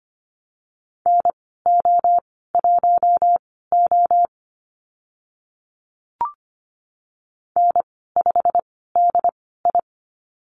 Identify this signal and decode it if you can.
Morse code, heard as a keyed tone, about 25 wpm, spaced as for 16 wpm, 700 Hz. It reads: NO1O N5DI
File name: Lesson 15g - Number 5 - Call Signs 25wpm